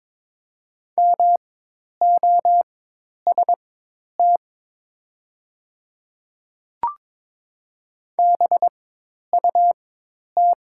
Morse code, heard as a keyed tone, about 22 wpm, spaced as for 10 wpm, 700 Hz. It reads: MOST BUT
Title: 4x Character Spacing - Top 200 Words - 22wpm